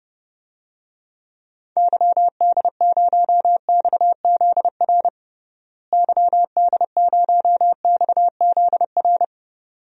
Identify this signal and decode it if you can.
Morse code, heard as a keyed tone, about 32 wpm, 700 Hz. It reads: YD0XZR YD0XZR
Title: DXCC-100 Call Signs - 6 Characters - Repeated 2x - 30wpm